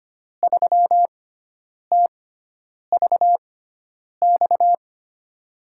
Morse code, heard as a keyed tone, about 25 wpm, spaced as for 9 wpm, 700 Hz. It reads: 3TVX
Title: USA Call Signs - 6x Character Spacing - 25wpm